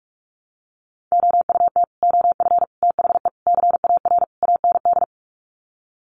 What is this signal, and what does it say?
Morse code, heard as a keyed tone, about 44 wpm, 700 Hz. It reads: OUT OF THE CAR AND
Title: Sets of 5 Words Encore 45wpm